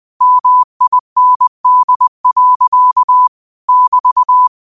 Morse code, heard as a keyed tone, about 20 wpm, 1000 Hz. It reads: MIND. <BT>
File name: qotdfiles / 20wpm